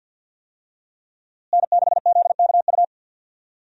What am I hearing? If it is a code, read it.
Morse code, heard as a keyed tone, about 51 wpm, 700 Hz. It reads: N6ZKU
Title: USA Call Signs - 5 Characters - Repeated 2x - 50wpm